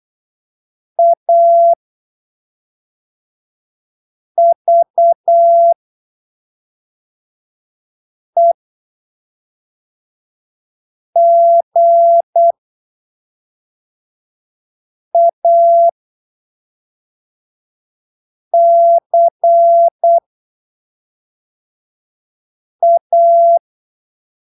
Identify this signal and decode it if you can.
Morse code, heard as a keyed tone, about 8 wpm, spaced as for 3 wpm, 680 Hz. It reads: AVEGACA